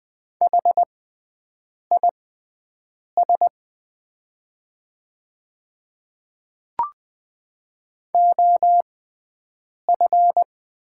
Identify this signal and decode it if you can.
Morse code, heard as a keyed tone, about 20 wpm, spaced as for 7 wpm, 700 Hz. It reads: HIS OF
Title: 6x Character Spacing - Top 100 Words - 20wpm